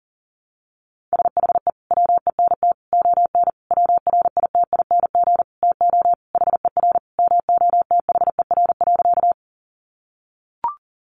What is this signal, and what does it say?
Morse code, heard as a keyed tone, about 40 wpm, 700 Hz. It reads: SHE WENT ON WRITING TO HER MOTHER.
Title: Sentences from Top 800 Words 40wpm